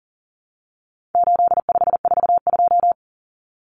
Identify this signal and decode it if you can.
Morse code, heard as a keyed tone, about 40 wpm, 700 Hz. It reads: 8542
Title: Common Cut Numbers - 4-Digit 40wpm